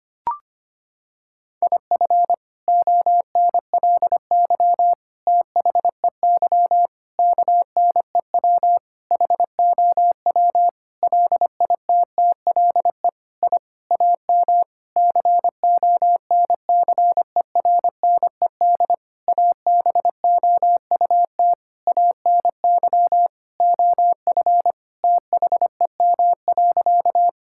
Morse code, heard as a keyed tone, about 25 wpm, 700 Hz. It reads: IF ONLY THEY KNEW HOW LITTLE I AM CONCERNED ABOUT ANY OF THEM.
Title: Sentences from Top 1000 Words - Review Part 6 - 25wpm